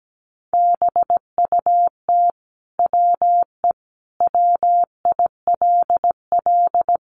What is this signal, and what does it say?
Morse code, heard as a keyed tone, about 17 wpm, 700 Hz. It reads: BUT WE WILL
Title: Sentences from Top 1200 Words 17wpm